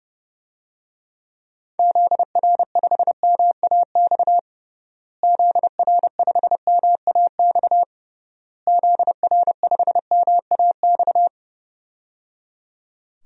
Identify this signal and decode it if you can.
Morse code, heard as a keyed tone, about 30 wpm, 700 Hz. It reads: ZR5MAX ZR5MAX ZR5MAX
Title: DXCC-100 Call Signs - 6 Characters - Repeated 3x - 30wpm